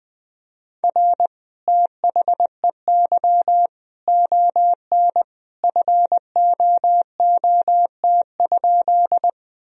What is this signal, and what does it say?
Morse code, heard as a keyed tone, about 20 wpm, 700 Hz. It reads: R THEY ON FOOT?
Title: Sentences from Top 500 Words and Abbreviations 20wpm